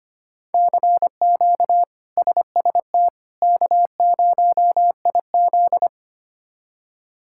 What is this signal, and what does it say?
Morse code, heard as a keyed tone, about 25 wpm, 700 Hz. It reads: CQ SST K0IZ
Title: K1USN SST QSO Practice - 25wpm